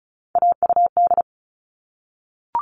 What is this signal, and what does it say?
Morse code, heard as a keyed tone, about 34 wpm, 700 Hz. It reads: AUD
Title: All Cut Numbers - 3-Digit 35wpm